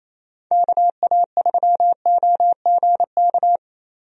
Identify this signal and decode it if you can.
Morse code, heard as a keyed tone, about 28 wpm, 700 Hz. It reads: KA3OGK
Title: USA Call Signs - 6 Characters - Repeated 3x - 28wpm